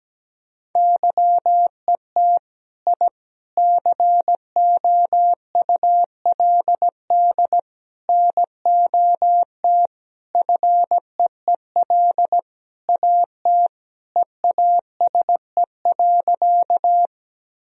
Morse code, heard as a keyed tone, about 17 wpm, 700 Hz. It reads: YET I COULD NOT FEEL AT EASE.
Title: Sentences from Top 1000 Words - Review Part 1 - 17wpm